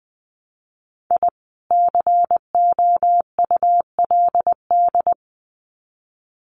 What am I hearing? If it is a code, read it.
Morse code, heard as a keyed tone, about 20 wpm, 700 Hz. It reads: I COULD
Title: Sets of 2 Words 20wpm